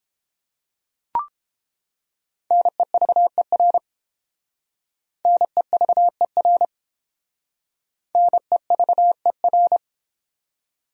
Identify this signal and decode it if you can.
Morse code, heard as a keyed tone, about 28 wpm, 700 Hz. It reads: NEVER NEVER NEVER